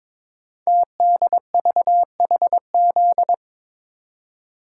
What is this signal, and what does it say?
Morse code, heard as a keyed tone, about 22 wpm, 700 Hz. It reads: TDVHZ
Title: Code Groups - 5 Letters - 22wpm